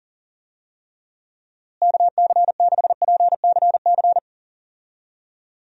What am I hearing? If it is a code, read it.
Morse code, heard as a keyed tone, about 40 wpm, 700 Hz. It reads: KC6PCC
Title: USA Call Signs - 6 Characters - Repeated 3x - 40wpm